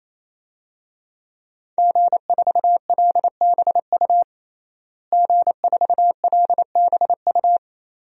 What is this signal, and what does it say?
Morse code, heard as a keyed tone, about 28 wpm, 700 Hz. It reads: G4LBU G4LBU
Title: DXCC-100 Call Signs - 5 Characters - Repeated 3x - 28wpm